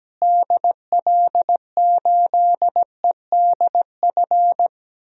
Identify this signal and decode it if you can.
Morse code, heard as a keyed tone, about 17 wpm, 700 Hz. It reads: DL8EDF
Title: DXCC-100 Call Signs - 6 Characters - Repeated 3x - 17wpm